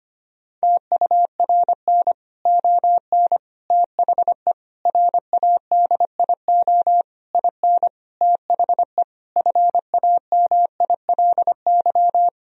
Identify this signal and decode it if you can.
Morse code, heard as a keyed tone, about 25 wpm, 700 Hz. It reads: TURN ON THE RADIO IN THE FAMILY
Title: Sentences from Top 1000 Words - Review Part 3 - 25wpm